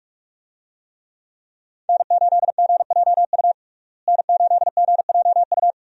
Morse code, heard as a keyed tone, about 46 wpm, 700 Hz. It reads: N8GJU N8GJU
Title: USA Call Signs - 5 Characters - Repeated 2x - 45wpm